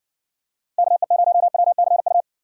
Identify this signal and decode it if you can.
Morse code, heard as a keyed tone, about 59 wpm, 700 Hz. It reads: KE0WCU